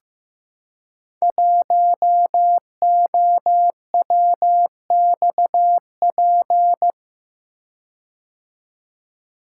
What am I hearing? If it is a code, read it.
Morse code, heard as a keyed tone, about 15 wpm, 700 Hz. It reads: 1OWXP
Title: Code Groups - 5 Letter:Numbers - 15wpm